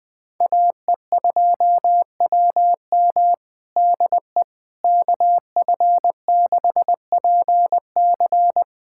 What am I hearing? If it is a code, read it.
Morse code, heard as a keyed tone, about 21 wpm, 700 Hz. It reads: AE2WM DE KF6PC
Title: USA Callsign DE Callsign - 5 Characters - 20wpm